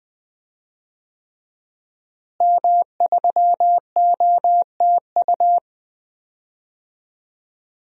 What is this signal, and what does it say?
Morse code, heard as a keyed tone, about 20 wpm, 700 Hz. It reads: M3OTU
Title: DXCC-100 Call Signs - 5 Characters - Repeated 3x - 20wpm